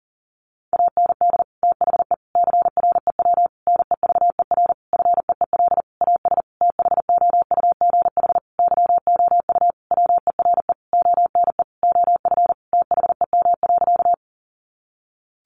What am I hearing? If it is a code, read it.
Morse code, heard as a keyed tone, about 40 wpm, 700 Hz. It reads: AND THE CREW NEVER FEEL AS THOUGH YOU WERE ONE OF THEM.
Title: Sentences from Top 1500 Words 40wpm